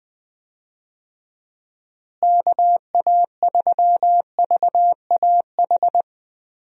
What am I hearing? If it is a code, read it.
Morse code, heard as a keyed tone, about 21 wpm, 700 Hz. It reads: KA3VAH
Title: USA Call Signs - 6 Characters - Repeated 2x - 20wpm